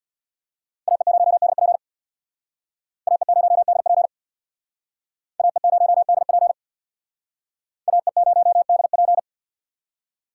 Morse code, heard as a keyed tone, about 61 wpm, 700 Hz. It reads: AE0DP AE0DP AE0DP AE0DP
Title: Speed Racing - USA 2x2 Call Signs 50wpm